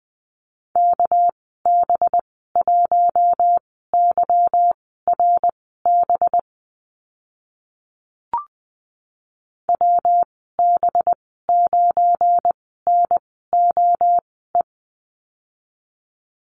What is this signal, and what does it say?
Morse code, heard as a keyed tone, about 19 wpm, spaced as for 14 wpm, 700 Hz. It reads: KB1YRB WB9NOE
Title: 245 Lesson 39g - Letter z - Call Signs 20wpm